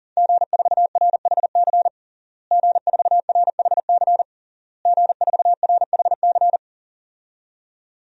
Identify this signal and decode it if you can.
Morse code, heard as a keyed tone, about 40 wpm, 700 Hz. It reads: G4RHC G4RHC G4RHC
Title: DXCC-100 Call Signs - 5 Characters - Repeated 3x - 40wpm